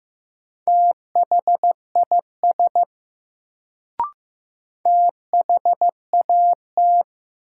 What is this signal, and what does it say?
Morse code, heard as a keyed tone, about 15 wpm, 700 Hz. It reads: THIS THAT
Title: Sets of 2 Words - New Word Drill - 15wpm